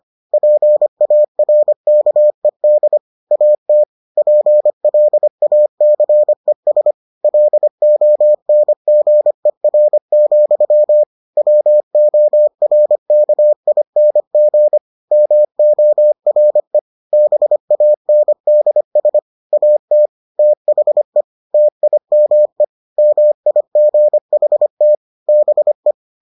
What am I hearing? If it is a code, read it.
Morse code, heard as a keyed tone, about 25 wpm, 600 Hz. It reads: PARKED AT PLACES LONGER, WORKING MORE BANDS AT THE TIME MIGHT BE